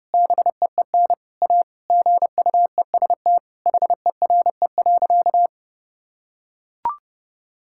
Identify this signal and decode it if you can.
Morse code, heard as a keyed tone, about 30 wpm, 700 Hz. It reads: BEEN A GUEST HERE.
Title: Sentences from Top 1600 Words 30wpm